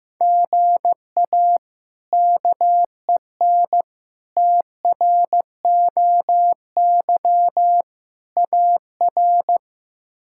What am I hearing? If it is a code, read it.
Morse code, heard as a keyed tone, about 16 wpm, 700 Hz. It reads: GA KEN TROY AR
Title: K1USN SST QSO Practice - 15wpm